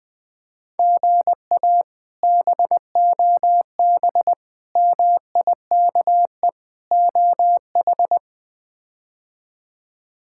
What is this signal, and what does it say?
Morse code, heard as a keyed tone, about 21 wpm, 700 Hz. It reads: GA BOB MIKE OH
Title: K1USN SST QSO Practice - 20wpm